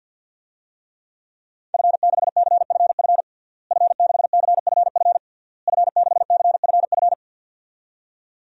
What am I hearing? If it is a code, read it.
Morse code, heard as a keyed tone, about 52 wpm, 700 Hz. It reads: F6CFF F6CFF F6CFF
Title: DXCC-100 Call Signs - 5 Characters - Repeated 3x - 50wpm